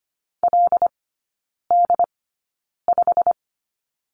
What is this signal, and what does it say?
Morse code, heard as a keyed tone, about 25 wpm, spaced as for 9 wpm, 700 Hz. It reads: LD5
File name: Lesson 15e - Number 5 - Three Characters 25wpm